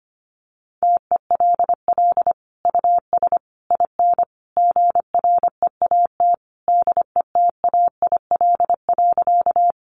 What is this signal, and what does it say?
Morse code, heard as a keyed tone, about 24 wpm, 700 Hz. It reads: TELL US IN GREAT DETAIL.
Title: Sentences from Top 1500 Words 25wpm